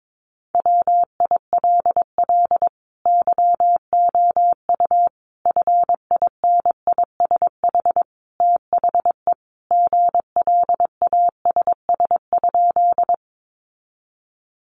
Morse code, heard as a keyed tone, about 22 wpm, 700 Hz. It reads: WILL YOU FINISH THE GLASS?
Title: Sentences from Top 1300 Words 22wpm